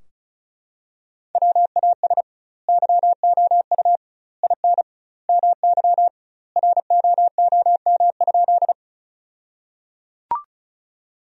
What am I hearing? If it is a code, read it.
Morse code, heard as a keyed tone, about 36 wpm, 700 Hz. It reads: WAS YOU IN MY ROOM?
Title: 2x Word-Spacing - Sentences from Top 300 Words - 35wpm